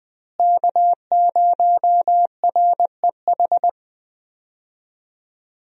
Morse code, heard as a keyed tone, about 20 wpm, 700 Hz. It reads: K0REH
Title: USA Call Signs - 5 Characters - Repeated 2x - 20wpm